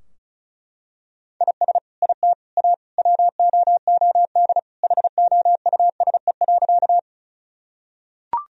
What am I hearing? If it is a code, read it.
Morse code, heard as a keyed tone, about 36 wpm, 700 Hz. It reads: IS IT A WOOD HOUSE.
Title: Sentences from Top 1000 Words - Review Part 2 - 35wpm